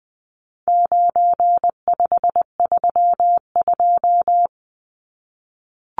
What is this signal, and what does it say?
Morse code, heard as a keyed tone, about 20 wpm, 700 Hz. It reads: 9532
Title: Numbers - 4-Digit 20wpm